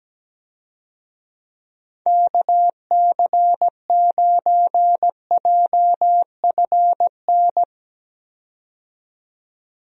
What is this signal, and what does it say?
Morse code, heard as a keyed tone, about 17 wpm, 700 Hz. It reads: KC9JFN
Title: USA Call Signs - 6 Characters - 17wpm